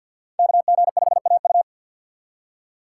Morse code, heard as a keyed tone, about 52 wpm, 700 Hz. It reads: KG5AU
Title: USA Call Signs - 5 Characters - 6x Character Spacing - 50wpm